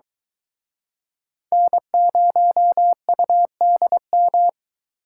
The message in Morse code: N0UDM

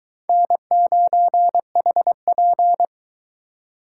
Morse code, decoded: N9HP

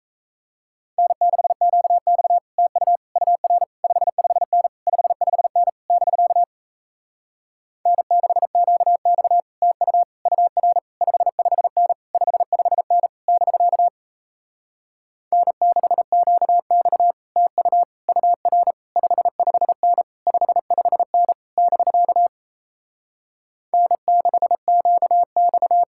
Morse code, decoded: N6QX TU UR 55N 55N <BK> N6QX TU UR 55N 55N <BK> N6QX TU UR 55N 55N <BK> N6QX

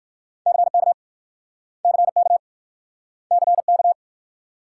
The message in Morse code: CK CK CK

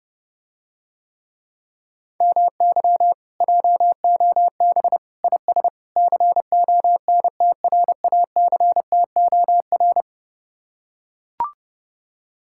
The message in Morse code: MY JOB IS CONTRACTOR

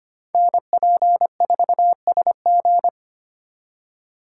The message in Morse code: NP4SG

